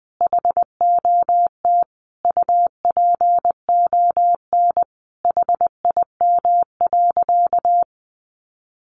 Morse code, heard as a keyed tone, about 20 wpm, 700 Hz. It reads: HOT UPON HIM.
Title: Sentences from Top 1100 Words 20wpm